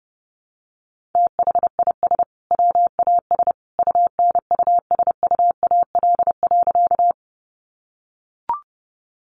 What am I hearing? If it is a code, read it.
Morse code, heard as a keyed tone, about 30 wpm, 700 Hz. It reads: THIS WAS UNUSUAL.
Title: Sentences from Top 1400 Words 30wpm